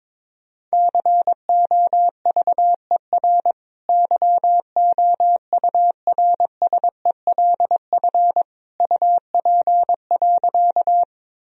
Morse code, heard as a keyed tone, about 22 wpm, 700 Hz. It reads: COVER YOURSELF UP.
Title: Sentences from Top 1000 Words - Review Part 1 - 22wpm